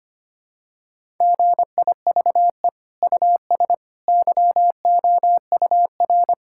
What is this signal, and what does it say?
Morse code, heard as a keyed tone, about 25 wpm, 700 Hz. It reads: GIVE US YOUR
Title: Sentences from Top 1000 Words - Review Part 5 - 25wpm